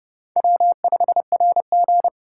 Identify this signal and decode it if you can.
Morse code, heard as a keyed tone, about 31 wpm, 700 Hz. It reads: W5RG
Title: USA Call Signs - 2x Character Spacing - 30wpm